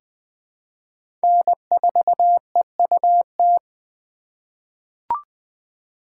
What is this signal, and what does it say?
Morse code, heard as a keyed tone, about 20 wpm, 700 Hz. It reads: N4EUT
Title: USA Call Signs - 4x Character Spacing - 20wpm